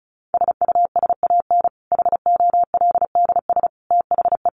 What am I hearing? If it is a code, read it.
Morse code, heard as a keyed tone, about 34 wpm, 700 Hz. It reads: SUSAN HOLDS THE
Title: Easy Phrases 35wpm